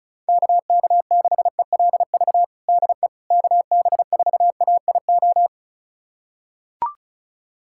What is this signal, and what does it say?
Morse code, heard as a keyed tone, about 34 wpm, 700 Hz. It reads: KK6ELV DE KB4AIO
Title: USA Callsign DE Callsign - 6 Characters - 35wpm